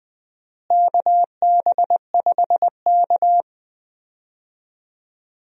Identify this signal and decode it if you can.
Morse code, heard as a keyed tone, about 21 wpm, 700 Hz. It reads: KB5K
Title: USA Call Signs - 4 Characters - Repeated 3x - 20wpm